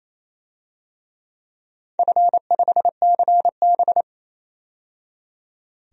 Morse code, heard as a keyed tone, about 28 wpm, 700 Hz. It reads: F5CB
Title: DXCC-100 Call Signs - 4 Characters - 28wpm